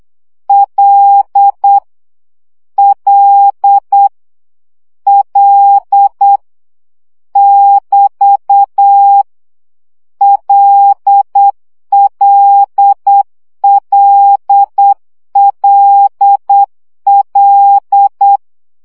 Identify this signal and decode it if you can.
Morse code, heard as a keyed tone, about 8 wpm, 800 Hz. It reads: L L L <BT> LLLLL
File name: CW gyakorlas l 7 baud